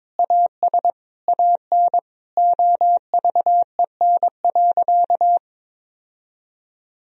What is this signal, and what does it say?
Morse code, heard as a keyed tone, about 22 wpm, 700 Hz. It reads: AS AN OVEN.
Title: Sentences from Top 1000 Words - Review Part 4 - 22wpm